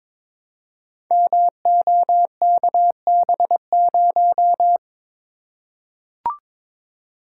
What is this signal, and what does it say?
Morse code, heard as a keyed tone, about 22 wpm, 700 Hz. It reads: MOKB0
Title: Code Groups - 5 Letter:Numbers - 22wpm